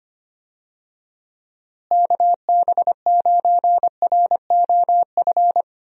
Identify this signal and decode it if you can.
Morse code, heard as a keyed tone, about 25 wpm, 700 Hz. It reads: KB9ROF
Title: USA Call Signs - 6 Characters - Repeated 3x - 25wpm